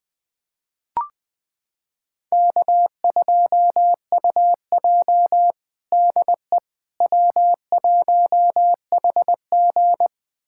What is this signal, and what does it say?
Morse code, heard as a keyed tone, about 20 wpm, 700 Hz. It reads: K2UJ DE W1HG
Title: USA Callsign DE Callsign - 4 Characters - 20wpm